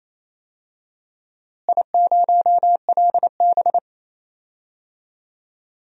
Morse code, heard as a keyed tone, about 28 wpm, 700 Hz. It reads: I0LB
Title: DXCC-100 Call Signs - 4 Characters - Repeated 2x - 28wpm